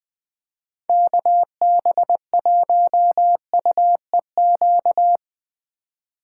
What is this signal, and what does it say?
Morse code, heard as a keyed tone, about 21 wpm, 700 Hz. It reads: KB1UEQ
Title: Speed Racing - USA 2x3 Call Signs 17wpm